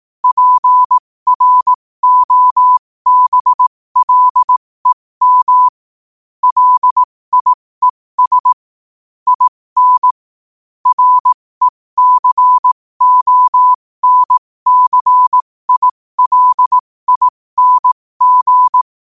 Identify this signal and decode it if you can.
Morse code, heard as a keyed tone, about 18 wpm, spaced as for 15 wpm, 1000 Hz. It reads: PROBLEM LIES IN RECONCILING
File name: qotdfiles / 15wpm